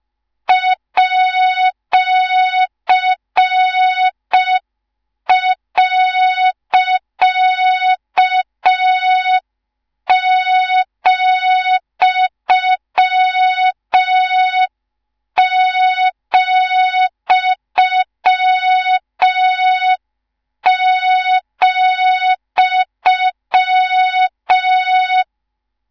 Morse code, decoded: @.,,,